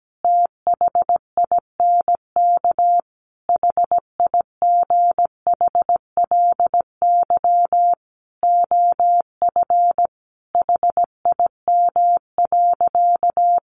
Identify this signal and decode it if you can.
Morse code, heard as a keyed tone, about 17 wpm, 700 Hz. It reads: THINK HIGHLY OF HIM.
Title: Sentences from Top 1100 Words 17wpm